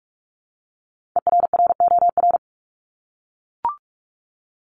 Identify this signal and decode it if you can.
Morse code, heard as a keyed tone, about 44 wpm, 700 Hz. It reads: ERROR
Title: Top 1751-2000 Words 45wpm